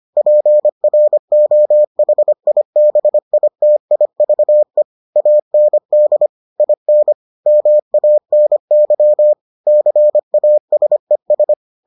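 Morse code, heard as a keyed tone, about 25 wpm, 600 Hz. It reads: PROHIBITIVE AND IN MANY CASES
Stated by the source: Morse edition of the Foundations of Amateur Radio episodes.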